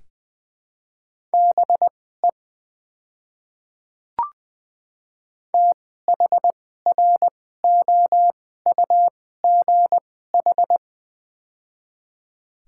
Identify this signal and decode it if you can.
Morse code, heard as a keyed tone, about 20 wpm, spaced as for 14 wpm, 700 Hz. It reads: BE THROUGH
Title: 2x Character Spacing - Top 300 Words - 20wpm